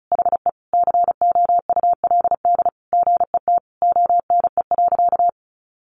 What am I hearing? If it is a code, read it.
Morse code, heard as a keyed tone, about 34 wpm, 700 Hz. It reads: HE COULD GET ONE.
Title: Sentences from Top 1400 Words 35wpm